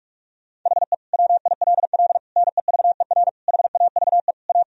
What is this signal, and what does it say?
Morse code, heard as a keyed tone, about 46 wpm, 700 Hz. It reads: HE WILL NEVER HAVE A
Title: Sentences from Top 1000 Words - Review Part 3 - 45wpm